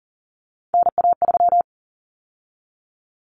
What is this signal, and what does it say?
Morse code, heard as a keyed tone, about 40 wpm, 700 Hz. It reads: NA3
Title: Common Cut Numbers - 3-Digit 40wpm